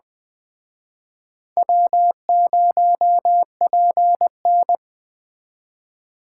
Morse code, heard as a keyed tone, about 21 wpm, 700 Hz. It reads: W0PN